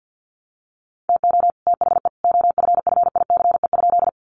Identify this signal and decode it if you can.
Morse code, heard as a keyed tone, about 48 wpm, 700 Hz. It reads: TO THE OFFICE?
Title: Sentences from Top 1000 Words 50wpm